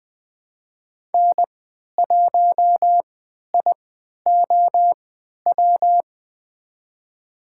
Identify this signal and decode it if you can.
Morse code, heard as a keyed tone, about 21 wpm, spaced as for 11 wpm, 700 Hz. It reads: N1IOW
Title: USA Call Signs - 5 Characters - 3x Character Spacing - 20wpm